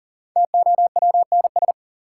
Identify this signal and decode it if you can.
Morse code, heard as a keyed tone, about 38 wpm, 700 Hz. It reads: TOWNS
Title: Sentences from Top 1000 Words - Review Part 6 - New Word Drill - 40wpm